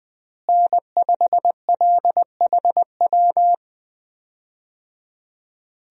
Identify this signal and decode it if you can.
Morse code, heard as a keyed tone, about 20 wpm, 700 Hz. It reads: N5LHW
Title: USA Call Signs - 5 Characters - Repeated 3x - 20wpm